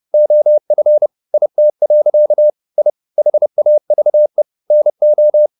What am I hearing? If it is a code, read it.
Morse code, heard as a keyed tone, about 30 wpm, 600 Hz. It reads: OF IT. I HAVE NO